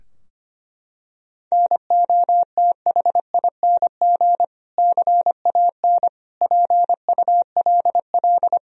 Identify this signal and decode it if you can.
Morse code, heard as a keyed tone, about 25 wpm, 700 Hz. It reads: NOTHING CAN PULL
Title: Sentences from Top 1000 Words - Review Part 2 - 25wpm